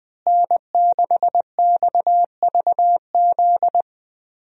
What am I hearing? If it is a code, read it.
Morse code, heard as a keyed tone, about 20 wpm, 700 Hz. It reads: N6XVZ
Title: USA Call Signs - 5 Characters - Repeated 3x - 20wpm